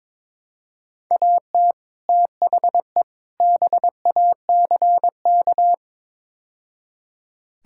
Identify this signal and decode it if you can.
Morse code, heard as a keyed tone, about 21 wpm, 700 Hz. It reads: AT THE BACK
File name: practice / 22wpm / Build A Phrase - 5-Word Phrases - 22wpm